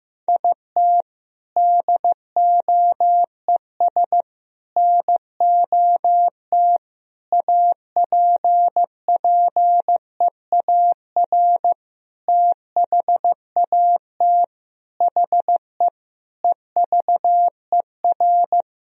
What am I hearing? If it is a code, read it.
Morse code, heard as a keyed tone, about 15 wpm, 700 Hz. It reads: IT DOES NOT APPEAR THAT HE EVER
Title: Sentences from Top 1000 Words - Review Part 2 - 15wpm